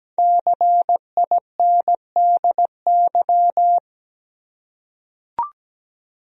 Morse code, heard as a keyed tone, about 17 wpm, 700 Hz. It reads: CINDY
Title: USA Common Names - Max 5 Letters - 17wpm